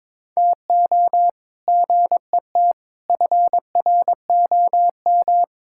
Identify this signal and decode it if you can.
Morse code, heard as a keyed tone, about 21 wpm, 700 Hz. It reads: TO GET FROM
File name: practice / 22wpm / Sentences with 8-Word Phrases - 22wpm